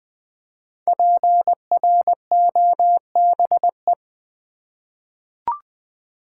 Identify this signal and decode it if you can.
Morse code, heard as a keyed tone, about 20 wpm, 700 Hz. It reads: PROBE
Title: 5 Letter Words - First 3 Letters Given - 20wpm